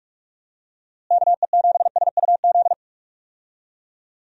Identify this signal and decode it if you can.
Morse code, heard as a keyed tone, about 46 wpm, 700 Hz. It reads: KE7SUZ
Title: USA Call Signs - 6 Characters - Repeated 2x - 45wpm